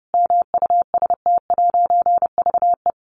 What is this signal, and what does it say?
Morse code, heard as a keyed tone, about 30 wpm, 700 Hz. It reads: MUST'VE
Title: A Sound Of Thunder 30wpm